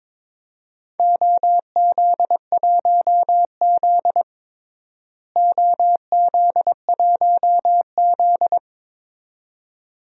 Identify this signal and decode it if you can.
Morse code, heard as a keyed tone, about 22 wpm, 700 Hz. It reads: OZ1Z OZ1Z
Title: DXCC-100 Call Signs - 4 Characters - Repeated 2x - 22wpm